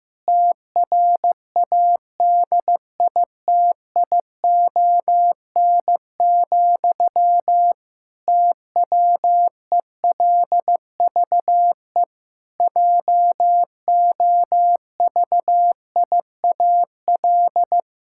Morse code, heard as a keyed tone, about 15 wpm, 700 Hz. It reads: TRADITION, TWELVE JOVIAL
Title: Pangram Challenge v2 - 15wpm